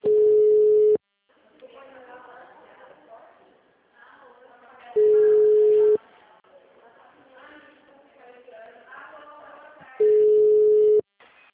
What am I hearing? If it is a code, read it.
Ringback tone (Europe, 425 Hz), with speech.